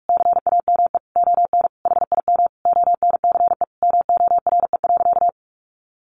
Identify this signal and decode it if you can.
Morse code, heard as a keyed tone, about 44 wpm, 700 Hz. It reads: CAME ON HIM ONCE MORE.
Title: Sentences from Top 1100 Words 45wpm